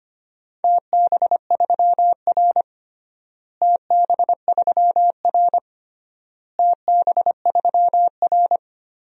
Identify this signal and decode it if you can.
Morse code, heard as a keyed tone, about 25 wpm, 700 Hz. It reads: TB3R TB3R TB3R
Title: DXCC-100 Call Signs - 4 Characters - Repeated 3x - 25wpm